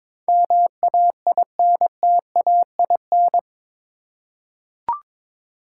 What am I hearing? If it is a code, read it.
Morse code, heard as a keyed tone, about 22 wpm, 700 Hz. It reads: MAINTAIN
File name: practice / 22wpm / Eight Letter Words - 22wpm